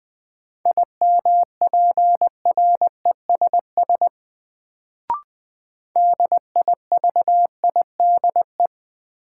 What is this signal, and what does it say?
Morse code, heard as a keyed tone, about 20 wpm, 700 Hz. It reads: IMPRESS DIVIDE
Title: Sentences from Top 1600 Words - New Word Drill - 20wpm